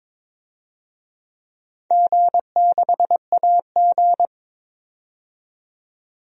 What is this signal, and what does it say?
Morse code, heard as a keyed tone, about 22 wpm, 700 Hz. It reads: G6AG
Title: DXCC-100 Call Signs - 4 Characters - 22wpm